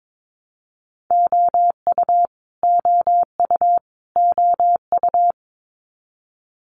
Morse code, heard as a keyed tone, about 22 wpm, 700 Hz. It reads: OU OU OU
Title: Multi-letter Phonemes Repeated 3x 22wpm